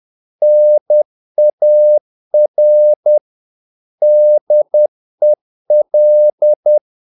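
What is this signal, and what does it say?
Morse code, heard as a keyed tone, about 10 wpm, 600 Hz. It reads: NAR DEL
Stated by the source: Morse practice of short Danish words at 10 wpm, from CW med Gnister.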